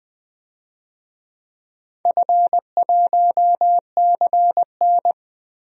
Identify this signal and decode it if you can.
Morse code, heard as a keyed tone, about 21 wpm, 700 Hz. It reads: F1CN
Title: DXCC-100 Call Signs - 4 Characters - Repeated 3x - 20wpm